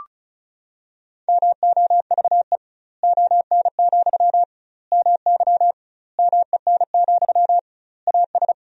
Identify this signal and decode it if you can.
Morse code, heard as a keyed tone, about 36 wpm, 700 Hz. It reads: MOVE ON, MY MEN, AS
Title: 2x Word-Spacing - Sentences from Top 500 Words - 35wpm